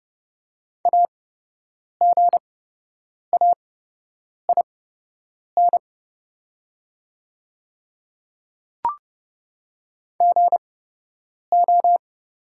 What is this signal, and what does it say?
Morse code, heard as a keyed tone, about 30 wpm, spaced as for 8 wpm, 700 Hz. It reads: AGAIN GO
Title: 8x Character Spacing - Top 300 Words - 30wpm